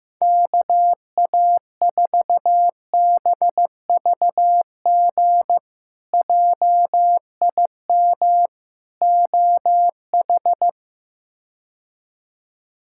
Morse code, heard as a keyed tone, about 15 wpm, 700 Hz. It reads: KA4BVG JIM OH
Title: K1USN SST QSO Practice - 15wpm